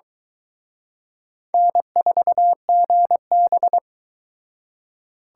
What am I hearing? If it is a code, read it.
Morse code, heard as a keyed tone, about 23 wpm, 700 Hz. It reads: N4GB